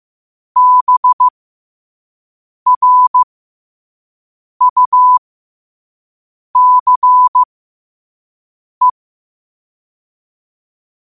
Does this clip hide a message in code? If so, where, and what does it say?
Yes: Morse code from 0.6 to 8.9 s, heard as a keyed tone, about 15 wpm, spaced as for 5 wpm, 1000 Hz. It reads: BRUCE